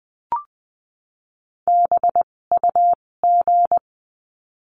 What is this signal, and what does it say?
Morse code, heard as a keyed tone, about 20 wpm, spaced as for 16 wpm, 700 Hz. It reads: BUG
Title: 256 Lesson 42a - Words 12-20wpm